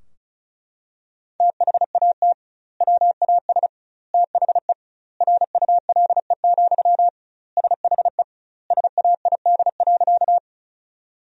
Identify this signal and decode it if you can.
Morse code, heard as a keyed tone, about 36 wpm, 700 Hz. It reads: THAT WAS THE RULE, SHE SAID.
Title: 2x Word-Spacing - Sentences from Top 400 Words - 35wpm